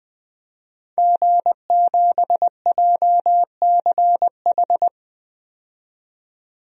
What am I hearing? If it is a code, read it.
Morse code, heard as a keyed tone, about 20 wpm, 700 Hz. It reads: G7JCH
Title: DXCC-100 Call Signs - 5 Characters - 20wpm